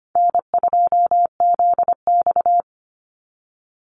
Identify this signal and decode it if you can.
Morse code, heard as a keyed tone, about 25 wpm, 700 Hz. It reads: N2ZX
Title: Call Signs Repeated 3x - 25wpm